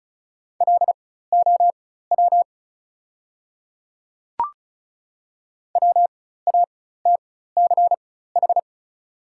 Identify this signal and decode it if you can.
Morse code, heard as a keyed tone, about 36 wpm, spaced as for 16 wpm, 690 Hz. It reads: LOW WATCH
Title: 4x Character Spacing - Top 300 Words - 35wpm